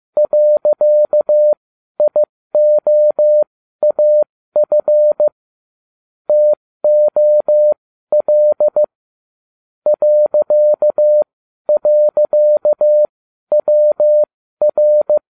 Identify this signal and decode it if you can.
Morse code, heard as a keyed tone, about 15 wpm, spaced as for 12 wpm, 600 Hz. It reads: .IOAF TOL ..WR